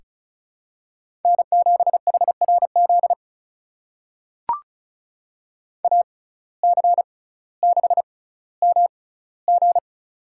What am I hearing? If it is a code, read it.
Morse code, heard as a keyed tone, about 34 wpm, spaced as for 18 wpm, 700 Hz. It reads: N7HRZ AC6MG
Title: USA Call Signs - 5 Characters - 6x Character Spacing - 35wpm